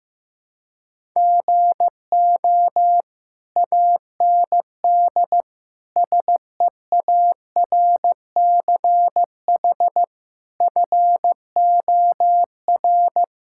Morse code, heard as a keyed tone, about 16 wpm, 700 Hz. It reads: GO AND SEARCH FOR
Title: Sentences from Top 1000 Words - Review Part 5 - 15wpm